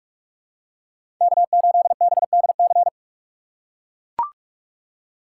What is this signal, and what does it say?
Morse code, heard as a keyed tone, about 43 wpm, 700 Hz. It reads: K8BDC
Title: USA Call Signs - 5 Characters - 4x Character Spacing - 45wpm